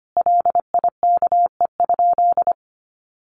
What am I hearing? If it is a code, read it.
Morse code, heard as a keyed tone, about 25 wpm, 700 Hz. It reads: LIKE?
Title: Sentences from Top 100 Words 25wpm